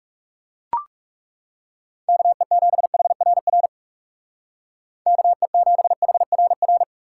Morse code, heard as a keyed tone, about 37 wpm, 700 Hz. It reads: KE7HRR KE7HRR
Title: Speed Racing - USA 2x3 Call Signs 30wpm